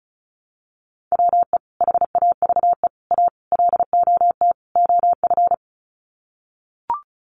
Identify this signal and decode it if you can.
Morse code, heard as a keyed tone, about 34 wpm, 700 Hz. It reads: WE HAVE A LOT OF
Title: Sets of 5 Words 35wpm